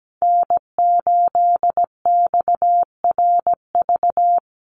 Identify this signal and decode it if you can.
Morse code, heard as a keyed tone, about 17 wpm, 700 Hz. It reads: N8XRV
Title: USA Call Signs Repeated 3x 17wpm